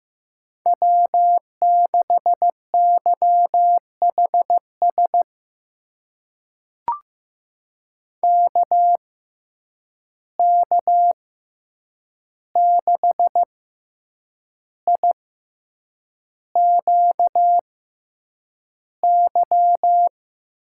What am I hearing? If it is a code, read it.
Morse code, heard as a keyed tone, about 15 wpm, spaced as for 7 wpm, 700 Hz. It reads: W6YHS KK6IQY
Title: USA Call Signs - 6x Character Spacing - 15wpm